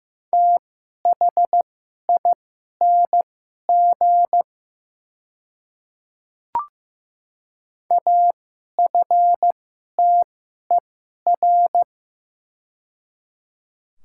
Morse code, heard as a keyed tone, about 15 wpm, spaced as for 11 wpm, 700 Hz. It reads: THING AFTER
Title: 2x Character Spacing - Top 200 Words - 15wpm